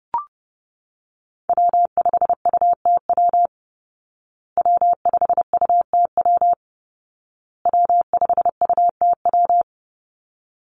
Morse code, heard as a keyed tone, about 30 wpm, 700 Hz. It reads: W5UTW W5UTW W5UTW
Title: Call Signs Repeated 2x - 30wpm